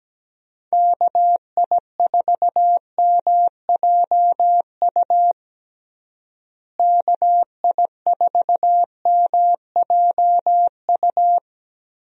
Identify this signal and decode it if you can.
Morse code, heard as a keyed tone, about 17 wpm, 700 Hz. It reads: KI4MJU KI4MJU
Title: USA Call Signs - 6 Characters - Repeated 3x - 17wpm